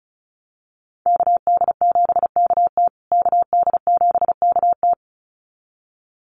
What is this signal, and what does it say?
Morse code, heard as a keyed tone, about 34 wpm, 700 Hz. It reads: KD7KT KD7KT
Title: USA Call Signs Repeated 2x 35wpm